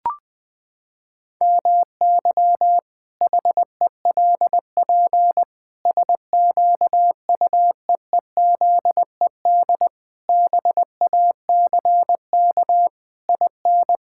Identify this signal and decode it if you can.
Morse code, heard as a keyed tone, about 21 wpm, 700 Hz. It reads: MY HELP SQUEEZED BACK IN
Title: Pangram Challenge v2 - 20wpm